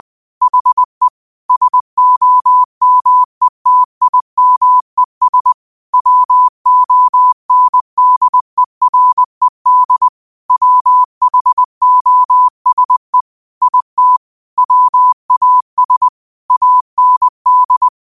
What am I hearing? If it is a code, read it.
Morse code, heard as a keyed tone, about 20 wpm, 1000 Hz. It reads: HE SOMETIMES WONDERED WHOSE IT WAS AND